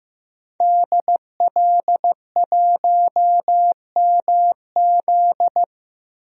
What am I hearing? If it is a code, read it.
Morse code, heard as a keyed tone, about 15 wpm, 700 Hz. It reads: DL1MZ
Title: DXCC-100 Call Signs - 5 Characters - 15wpm